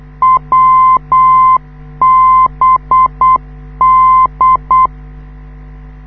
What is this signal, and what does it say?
Morse code, heard as a keyed tone, about 8 wpm, 1040 Hz. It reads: WBD
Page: this is how the NDB of the Wiesbaden US Army Airfield at Erbenheim sounds like, when you live 2 km away